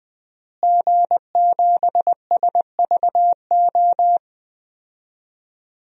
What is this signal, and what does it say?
Morse code, heard as a keyed tone, about 20 wpm, 700 Hz. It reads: G7SVO
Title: DXCC-100 Call Signs - 5 Characters - Repeated 3x - 20wpm